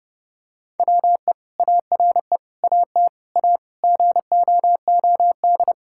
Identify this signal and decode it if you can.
Morse code, heard as a keyed tone, about 30 wpm, 700 Hz. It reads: WE ARE AT A GOOD
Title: Sentences from Top 1000 Words - Review Part 6 - 30wpm